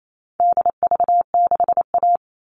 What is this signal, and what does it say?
Morse code, heard as a keyed tone, about 28 wpm, 700 Hz. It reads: DV6A
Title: All Cut Numbers - 4-Digit 28wpm